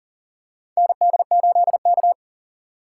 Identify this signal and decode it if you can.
Morse code, heard as a keyed tone, about 40 wpm, 700 Hz. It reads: ND8K
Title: USA Call Signs - 4x Character Spacing - 40wpm